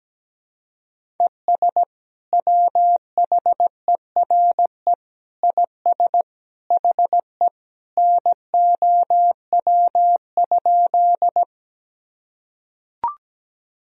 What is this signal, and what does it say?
Morse code, heard as a keyed tone, about 17 wpm, 700 Hz. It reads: ES WHERE IS HE NOW?
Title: Sentences from Top 200 Words and Abbreviations 17wpm